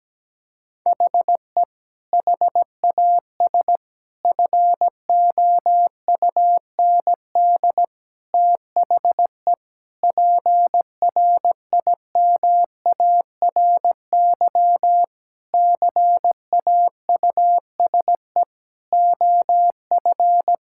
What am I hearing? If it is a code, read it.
Morse code, heard as a keyed tone, about 17 wpm, 700 Hz. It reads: HE HAS FOUND THE PRIMARY CAUSE OF
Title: Sentences from Top 1000 Words - Review Part 5 - 17wpm